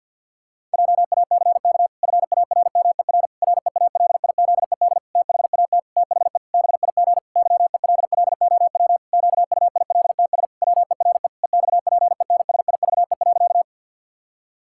Morse code, heard as a keyed tone, about 50 wpm, 700 Hz. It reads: JACK FARMER REALIZED THAT THE BIG YELLOW QUILTS WERE EXPENSIVE.
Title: Pangram Challenge v2 - 50wpm